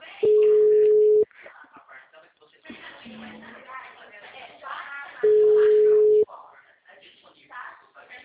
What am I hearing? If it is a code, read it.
Ringback tone (Europe, 425 Hz), with speech.